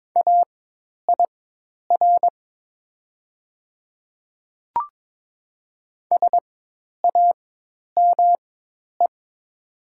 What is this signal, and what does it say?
Morse code, heard as a keyed tone, about 22 wpm, spaced as for 10 wpm, 700 Hz. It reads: AIR SAME
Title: 4x Character Spacing - Top 200 Words - 22wpm